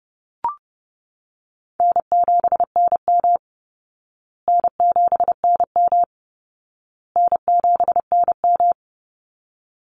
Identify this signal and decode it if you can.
Morse code, heard as a keyed tone, about 30 wpm, 700 Hz. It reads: N7NM N7NM N7NM
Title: Call Signs Repeated 2x - 30wpm